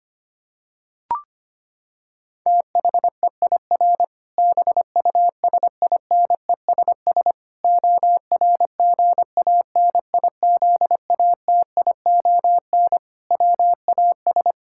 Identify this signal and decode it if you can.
Morse code, heard as a keyed tone, about 25 wpm, 700 Hz. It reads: THEIR BUSINESS ORGANIZATION WAS
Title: Sentences from Top 1000 Words - Review Part 5 - 25wpm